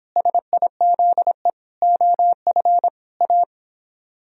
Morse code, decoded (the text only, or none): SIZE OF A